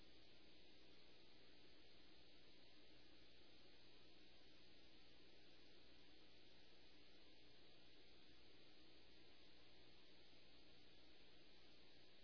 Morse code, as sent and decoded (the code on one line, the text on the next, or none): none